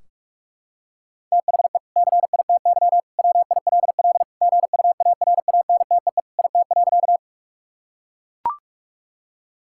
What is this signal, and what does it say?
Morse code, heard as a keyed tone, about 43 wpm, 700 Hz. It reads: THE CITY WILL GUARANTEE IT.
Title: Sentences from Top 1600 Words 45wpm